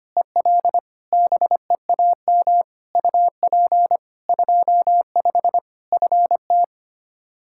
Morse code, heard as a keyed tone, about 25 wpm, 700 Hz. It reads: EL BEAM UP 25 FT
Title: Practice QSO - Exchange 2 - Info - 25wpm